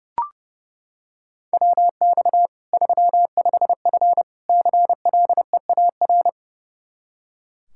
Morse code, heard as a keyed tone, about 30 wpm, 700 Hz. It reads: WX 35F CLEAR
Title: K1USN SST QSO Practice 30wpm